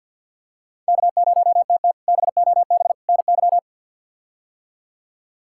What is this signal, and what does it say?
Morse code, heard as a keyed tone, about 52 wpm, 690 Hz. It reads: K0TT BOB NY
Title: K1USN SST QSO Practice - 50wpm